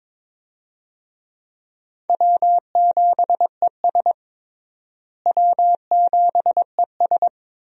Morse code, heard as a keyed tone, about 22 wpm, 700 Hz. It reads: W7ES W7ES
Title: USA Call Signs - 4 Characters - Repeated 2x - 22wpm